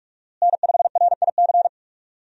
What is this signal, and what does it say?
Morse code, heard as a keyed tone, about 48 wpm, 700 Hz. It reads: N5RIC
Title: USA Call Signs - 5 Characters - Repeated 3x - 45wpm